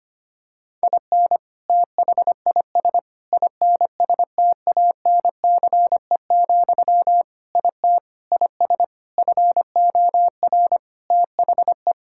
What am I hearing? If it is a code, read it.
Morse code, heard as a keyed tone, about 25 wpm, 700 Hz. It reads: IN THIS INSTANCE, IT IS FOR THE
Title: Sentences from Top 1000 Words - Review Part 5 - 25wpm